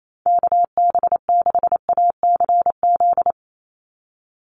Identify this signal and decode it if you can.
Morse code, heard as a keyed tone, about 28 wpm, 700 Hz. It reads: KB6ACZ
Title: USA Call Signs 28wpm